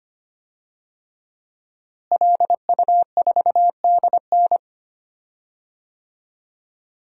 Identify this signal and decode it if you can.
Morse code, heard as a keyed tone, about 25 wpm, 700 Hz. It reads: LU4DN
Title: DXCC-100 Call Signs - 5 Characters - Repeated 3x - 25wpm